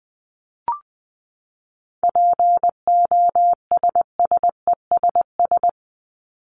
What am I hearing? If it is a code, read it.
Morse code, heard as a keyed tone, about 20 wpm, 700 Hz. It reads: POSSESS
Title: Top 1751-2000 Words 20wpm